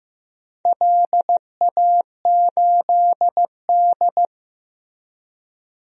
Morse code, heard as a keyed tone, about 15 wpm, 700 Hz. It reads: LA8D
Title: DXCC-100 Call Signs - 4 Characters - Repeated 2x - 15wpm